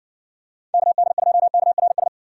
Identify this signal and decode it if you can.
Morse code, heard as a keyed tone, about 55 wpm, 700 Hz. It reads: KD2KRS